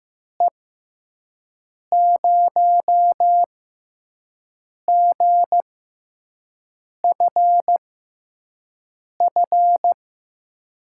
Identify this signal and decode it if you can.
Morse code, heard as a keyed tone, about 15 wpm, spaced as for 5 wpm, 700 Hz. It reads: E0GFF
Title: USA Call Signs - 6 Characters - 6x Character Spacing - 15wpm